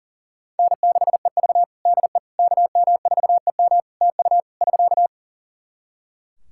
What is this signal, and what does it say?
Morse code, heard as a keyed tone, about 40 wpm, 700 Hz. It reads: N6EV DE KM4EM TU <SK>
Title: Practice QSO - Exchange 4 - Ending - 40wpm